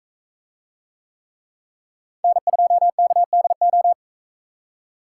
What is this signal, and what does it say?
Morse code, heard as a keyed tone, about 40 wpm, 700 Hz. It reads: N2KDO